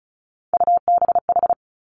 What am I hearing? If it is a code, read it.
Morse code, heard as a keyed tone, about 36 wpm, 700 Hz. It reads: UBH
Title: Three Challenging Letter-Number Combinations 35wpm